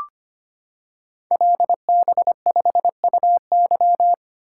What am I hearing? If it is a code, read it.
Morse code, heard as a keyed tone, about 25 wpm, 700 Hz. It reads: LB5UY
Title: Code Groups - 5 Letter:Numbers - 25wpm